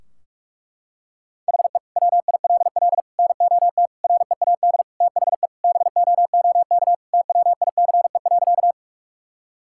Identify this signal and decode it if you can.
Morse code, heard as a keyed tone, about 46 wpm, 700 Hz. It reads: HE WILL NOT READ THE BOOK TWICE.
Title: Sentences from Top 1600 Words 45wpm